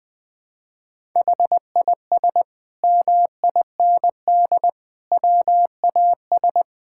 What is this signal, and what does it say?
Morse code, heard as a keyed tone, about 20 wpm, 700 Hz. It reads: HIS MIND WAS
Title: Sentences from Top 1000 Words - Review Part 6 - 20wpm